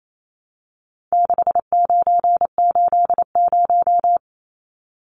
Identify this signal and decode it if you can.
Morse code, heard as a keyed tone, about 28 wpm, 700 Hz. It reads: 6980
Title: Numbers - 4-Digit 28wpm